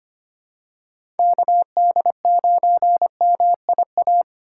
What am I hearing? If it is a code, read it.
Morse code, heard as a keyed tone, about 25 wpm, 700 Hz. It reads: KD9MIA
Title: USA Call Signs - 6 Characters - 8x Character Spacing - 25wpm